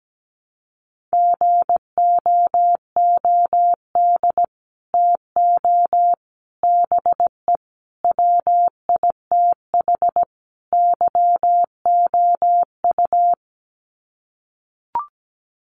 Morse code, heard as a keyed tone, about 17 wpm, 700 Hz. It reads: GOOD TO BE WITH YOU
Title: Sets of 5 Words Encore2 17wpm